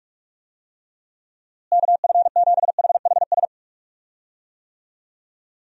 Morse code, heard as a keyed tone, about 47 wpm, 700 Hz. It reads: KF7HHS
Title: USA Call Signs - 6 Characters - Repeated 2x - 45wpm